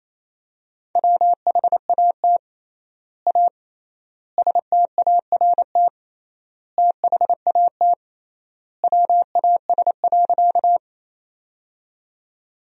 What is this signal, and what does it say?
Morse code, heard as a keyed tone, about 27 wpm, 700 Hz. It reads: WHAT A START THAT WAS.
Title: 3x Word-Spacing - Sentences from Top 300 Words - 28wpm